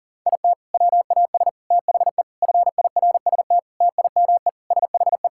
Morse code, heard as a keyed tone, about 40 wpm, 700 Hz. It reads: IT WAS THE FIRST TIME SHE
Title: Sentences from Top 1000 Words - Review Part 6 - 40wpm